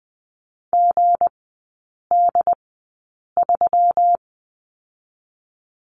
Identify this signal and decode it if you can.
Morse code, heard as a keyed tone, about 20 wpm, spaced as for 8 wpm, 700 Hz. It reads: GD3
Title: 184 Lesson 30e - Letter b - Three Characters 20wpm